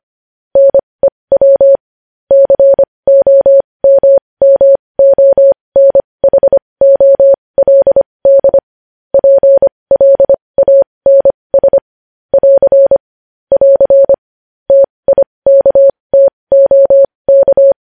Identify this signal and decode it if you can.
Morse code, heard as a keyed tone, about 25 wpm, spaced as for 20 wpm, 550 Hz. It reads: NEW COMMONHOLD PLANS <AR> <AR> TIKTOK
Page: news-headlines-in-morse-code-at-20-wpm